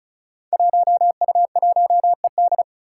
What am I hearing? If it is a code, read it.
Morse code, heard as a keyed tone, about 34 wpm, 700 Hz. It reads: 1U1ED
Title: Five Challenging Letter-Number Combinations 35wpm